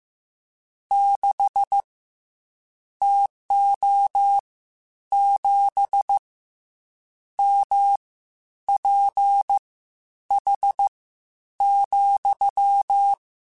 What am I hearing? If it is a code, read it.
Morse code, heard as a keyed tone, about 15 wpm, spaced as for 9 wpm, 800 Hz. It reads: 6 TO 7 MPH,
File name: Spaced Characters / Spaced Characters-mp3s / Weather 4 24 sp @13